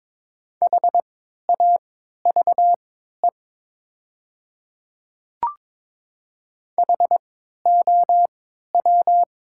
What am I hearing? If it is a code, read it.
Morse code, heard as a keyed tone, about 22 wpm, spaced as for 12 wpm, 700 Hz. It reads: HAVE HOW
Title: 3x Character Spacing - Top 100 Words - 22wpm